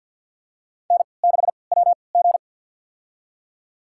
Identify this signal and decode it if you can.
Morse code, heard as a keyed tone, about 52 wpm, spaced as for 29 wpm, 700 Hz. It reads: N6WG
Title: USA Call Signs - 4 Characters - 3x Character Spacing - 50wpm